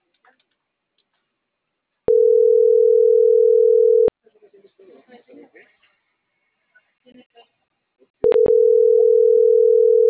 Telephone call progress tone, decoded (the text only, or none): ringback tone